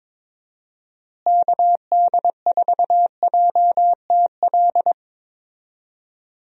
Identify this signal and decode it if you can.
Morse code, heard as a keyed tone, about 22 wpm, 700 Hz. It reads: KD4JTL
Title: USA Call Signs - 6 Characters - Repeated 2x - 22wpm